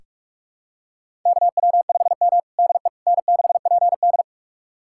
Morse code, heard as a keyed tone, about 42 wpm, 700 Hz. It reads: KW5M DE N6PD
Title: USA Callsign DE Callsign - 4 Characters - 45wpm